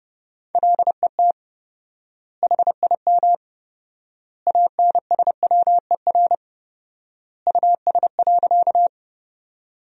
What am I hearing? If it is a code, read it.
Morse code, heard as a keyed tone, about 30 wpm, 700 Hz. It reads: LET HIM ANSWER US.
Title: 4x Word-Spacing - Sentences from Top 300 Words - 30wpm